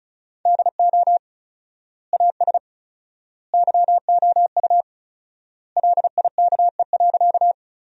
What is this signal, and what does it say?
Morse code, heard as a keyed tone, about 36 wpm, 700 Hz. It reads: DO AS YOU LIKE.
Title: 4x Word-Spacing - Sentences from Top 100 Words - 35wpm